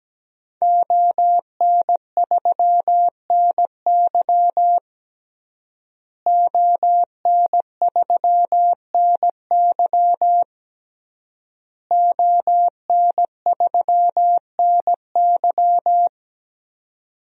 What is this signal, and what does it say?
Morse code, heard as a keyed tone, about 17 wpm, 700 Hz. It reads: ON3NY ON3NY ON3NY
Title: DXCC-100 Call Signs - 5 Characters - Repeated 3x - 17wpm